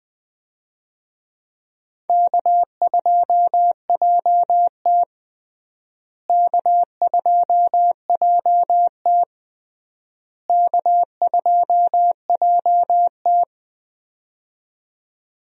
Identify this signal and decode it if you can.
Morse code, heard as a keyed tone, about 20 wpm, 700 Hz. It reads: K2JT K2JT K2JT
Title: USA Call Signs - 4 Characters - Repeated 3x - 20wpm